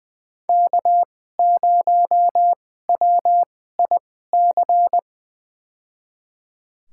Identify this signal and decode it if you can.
Morse code, heard as a keyed tone, about 20 wpm, spaced as for 15 wpm, 700 Hz. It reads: K0WIC
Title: USA Call Signs - 5 Characters - 2x Character Spacing - 20wpm